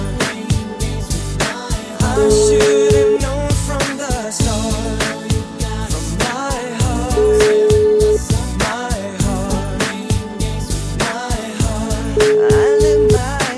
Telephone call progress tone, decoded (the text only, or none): ringback tone